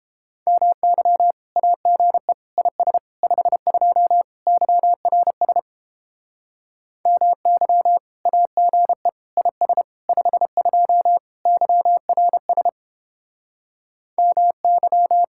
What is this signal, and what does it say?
Morse code, heard as a keyed tone, about 29 wpm, 700 Hz. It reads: MY AGE IS 52 YRS MY AGE IS 52 YRS MY